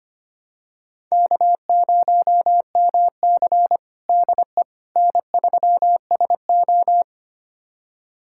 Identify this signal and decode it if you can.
Morse code, heard as a keyed tone, about 25 wpm, 700 Hz. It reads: K0MC DE N3SO
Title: USA Callsign DE Callsign - 4 Characters - 25wpm